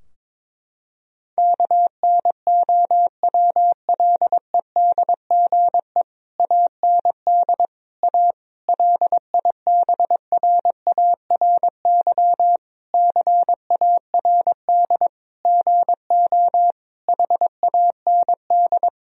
Morse code, heard as a keyed tone, about 22 wpm, 700 Hz. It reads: KNOWLEDGE AND A LIBRARY CARD GO HAND
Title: Sentences from Top 1000 Words - Review Part 4 - 22wpm